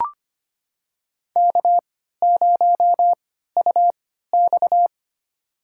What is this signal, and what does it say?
Morse code, heard as a keyed tone, about 25 wpm, spaced as for 14 wpm, 700 Hz. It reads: K0UX
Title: USA Call Signs - 4 Characters - 3x Character Spacing - 25wpm